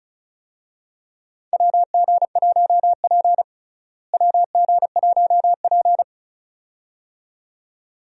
Morse code, heard as a keyed tone, about 36 wpm, 700 Hz. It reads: WG1P WG1P
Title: USA Call Signs - 4 Characters - Repeated 2x - 35wpm